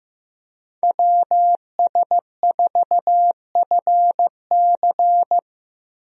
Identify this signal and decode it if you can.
Morse code, heard as a keyed tone, about 15 wpm, 700 Hz. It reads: WS4FC
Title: USA Call Signs - 5 Characters - Repeated 3x - 15wpm